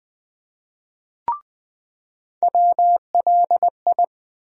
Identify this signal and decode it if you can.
Morse code, heard as a keyed tone, about 20 wpm, 700 Hz. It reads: WLI
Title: USA Call Sign Postfixes - 3 Letters 20wpm